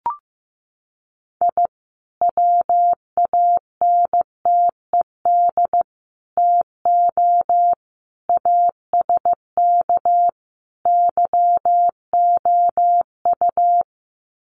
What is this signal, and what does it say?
Morse code, heard as a keyed tone, about 15 wpm, 700 Hz. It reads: I WANTED TO ASK YOU
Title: Sets of 5 Words Encore2 15wpm